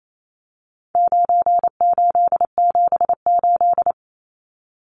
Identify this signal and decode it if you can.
Morse code, heard as a keyed tone, about 28 wpm, 700 Hz. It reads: 9878
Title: Numbers - 4-Digit 28wpm